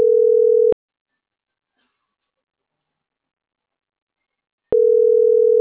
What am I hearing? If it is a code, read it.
Ringback tone (North America), 440 + 480 Hz.